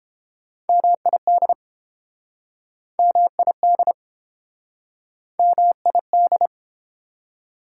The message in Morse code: MID MID MID

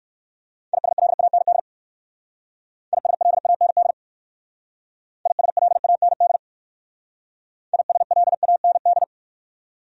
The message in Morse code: ISLAND ISLAND ISLAND ISLAND

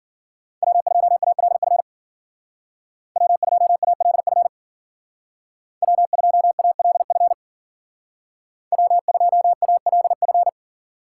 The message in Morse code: W2ALF W2ALF W2ALF W2ALF

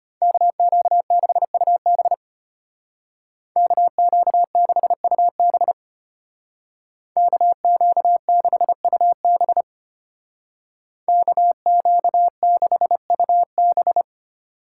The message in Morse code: KQ6UB KQ6UB KQ6UB KQ6UB

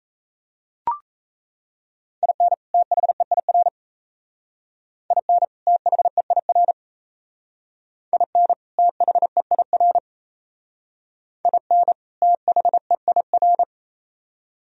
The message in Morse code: IN THEIR IN THEIR IN THEIR IN THEIR

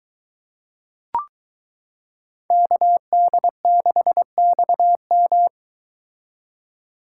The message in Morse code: KD6XM

